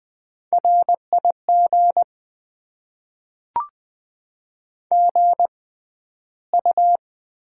Morse code, heard as a keyed tone, about 20 wpm, spaced as for 12 wpm, 700 Hz. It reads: RIG GU
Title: Top 20 QSO Elements - 6x Character Spacing - 20wpm